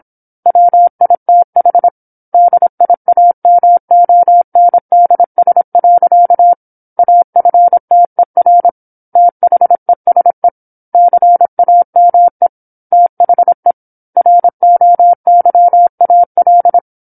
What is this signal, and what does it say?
Morse code, heard as a keyed tone, about 27 wpm, 700 Hz. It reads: WITH DIAMONDS. AFTER THESE CAME THE ROYAL
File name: 08-Chapter-08 26 WPM